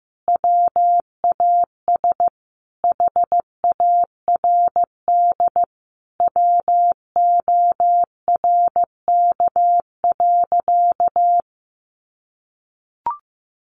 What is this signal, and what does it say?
Morse code, heard as a keyed tone, about 15 wpm, 700 Hz. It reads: WAS HARD WORK.
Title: Sentences from Top 600 Words 15wpm